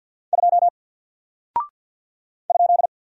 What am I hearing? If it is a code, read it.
Morse code, heard as a keyed tone, about 49 wpm, 700 Hz. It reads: 2 ?
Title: Single Character (Letter:Number:Punctuation:Prosign) 50wpm